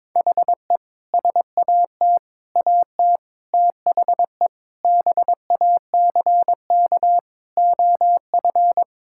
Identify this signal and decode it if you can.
Morse code, heard as a keyed tone, about 22 wpm, 700 Hz. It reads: HE SAT AT THE BACK OF
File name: practice / 22wpm / Sentences with 3-Word Phrases - 22wpm